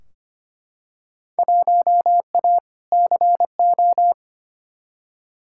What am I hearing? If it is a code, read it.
Morse code, heard as a keyed tone, about 25 wpm, 700 Hz. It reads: 1A CO
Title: Field Day Exchanges - 25wpm